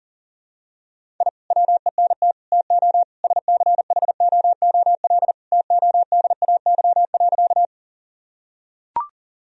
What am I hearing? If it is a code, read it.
Morse code, heard as a keyed tone, about 40 wpm, 700 Hz. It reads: I WENT TO SCHOOL TODAY.
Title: Sentences from Top 1000 Words - Review Part 1 - 40wpm